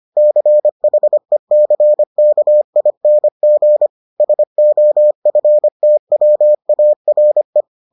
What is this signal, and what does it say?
Morse code, heard as a keyed tone, about 25 wpm, 600 Hz. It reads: CHECKING SOFTWARE